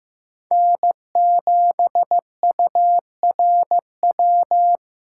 Morse code, heard as a keyed tone, about 18 wpm, 700 Hz. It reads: N7URW
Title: Speed Racing - USA 1x3 Call Signs 15wpm